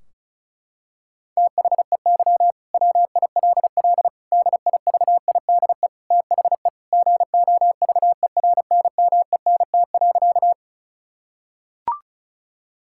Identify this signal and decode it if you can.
Morse code, heard as a keyed tone, about 34 wpm, 700 Hz. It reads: THEY WILL DIVIDE THE GOVERNMENT.
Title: Sentences from Top 1600 Words 35wpm